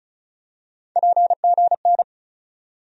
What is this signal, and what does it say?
Morse code, heard as a keyed tone, about 34 wpm, 700 Hz. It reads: PGN
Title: USA Call Sign Postfixes - 3 Letters 35wpm